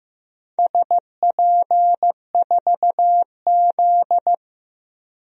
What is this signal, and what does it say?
Morse code, heard as a keyed tone, about 15 wpm, 700 Hz. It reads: SP4Z
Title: DXCC-100 Call Signs - 4 Characters - 15wpm